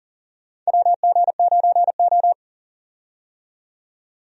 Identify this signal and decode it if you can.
Morse code, heard as a keyed tone, about 40 wpm, 700 Hz. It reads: WG9O
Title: USA Call Signs - 4 Characters - 40wpm